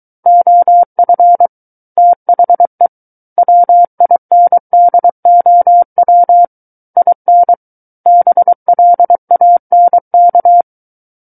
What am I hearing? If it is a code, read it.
Morse code, heard as a keyed tone, about 23 wpm, 700 Hz. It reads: OF THE WINDOW IN BLANK
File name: Book One / 14-Chapter 14 - 23 WPM